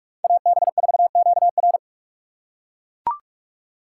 Morse code, heard as a keyed tone, about 42 wpm, 700 Hz. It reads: AB4QR